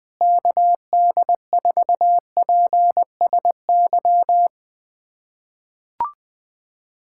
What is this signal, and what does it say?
Morse code, heard as a keyed tone, about 20 wpm, 700 Hz. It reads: KD4PSY
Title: USA Callsign DE Callsign 20wpm